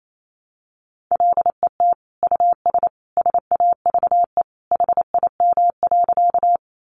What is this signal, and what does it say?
Morse code, heard as a keyed tone, about 28 wpm, 700 Hz. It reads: LET US SAVE HIM.
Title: Sentences from Top 1100 Words 28wpm